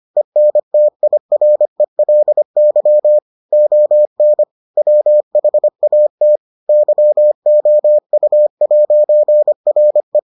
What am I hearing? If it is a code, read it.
Morse code, heard as a keyed tone, about 25 wpm, 600 Hz. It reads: ENTIRELY ON WHAT YOU'RE